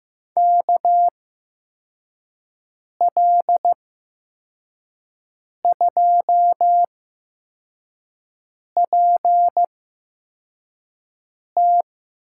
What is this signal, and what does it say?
Morse code, heard as a keyed tone, about 15 wpm, spaced as for 4 wpm, 700 Hz. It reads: KL2PT
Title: USA Call Signs - 5 Characters - 8x Character Spacing - 15wpm